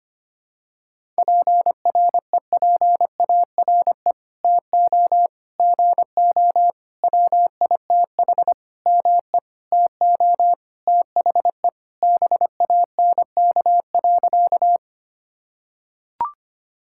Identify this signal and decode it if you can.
Morse code, heard as a keyed tone, about 25 wpm, 700 Hz. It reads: PREPARE TO GO WITH ME TO THE BANK.
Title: Sentences from Top 1000 Words - Review Part 6 - 25wpm